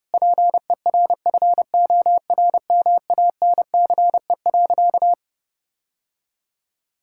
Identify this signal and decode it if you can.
Morse code, heard as a keyed tone, about 30 wpm, 700 Hz. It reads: PERFORMANCE.
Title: Sentences from Top 1600 Words 30wpm